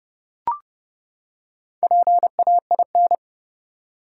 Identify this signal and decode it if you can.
Morse code, heard as a keyed tone, about 32 wpm, 700 Hz. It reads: PAIN
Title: Four Letter Words - 30wpm